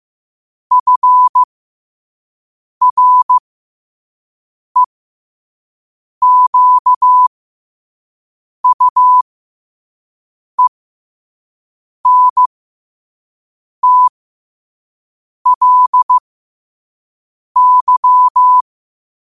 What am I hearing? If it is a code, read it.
Morse code, heard as a keyed tone, about 15 wpm, spaced as for 5 wpm, 1000 Hz. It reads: FREQUENTLY
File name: qotdfiles / 5wpm